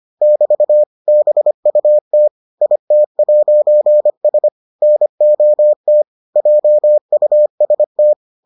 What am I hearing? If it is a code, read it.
Morse code, heard as a keyed tone, about 25 wpm, 600 Hz. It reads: <BT> BUT IT'S NOT JUST